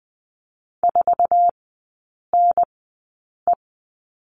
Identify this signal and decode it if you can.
Morse code, heard as a keyed tone, about 20 wpm, spaced as for 8 wpm, 700 Hz. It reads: 4NE
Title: 060 Lesson 11e - Letter h - Three Characters 20wpm